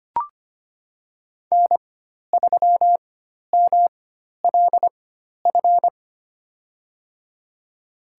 Morse code, heard as a keyed tone, about 25 wpm, spaced as for 12 wpm, 700 Hz. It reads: N3MLF
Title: USA Call Signs - 5 Characters - 4x Character Spacing - 25wpm